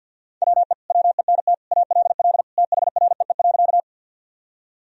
Morse code, heard as a keyed tone, about 47 wpm, 700 Hz. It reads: WE WENT ALL THREE.
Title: Sentences from Top 1000 Words - Review Part 1 - 50wpm